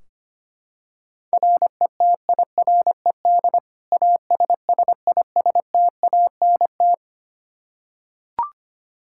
Morse code, heard as a keyed tone, about 25 wpm, 700 Hz. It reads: RETIRED ASSISTANT
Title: K1USN SST QSO Practice 25wpm